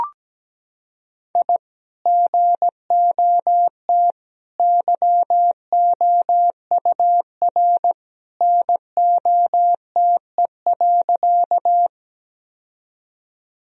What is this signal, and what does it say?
Morse code, heard as a keyed tone, about 17 wpm, 700 Hz. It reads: I GOT YOUR NOTE.
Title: Sentences from Top 1000 Words - Review Part 2 - 17wpm